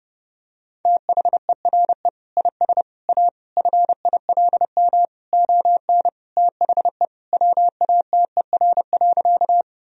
Morse code, heard as a keyed tone, about 30 wpm, 700 Hz. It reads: THERE IS A FILM ON THE WATER.
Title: Sentences from Top 1000 Words - Review Part 3 - 30wpm